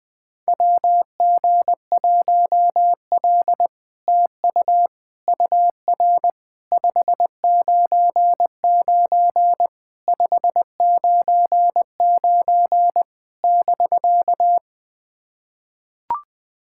Morse code, heard as a keyed tone, about 21 wpm, 700 Hz. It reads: WG1L TU UR 599 599 <BK>
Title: SOTA QSO Practice 20wpm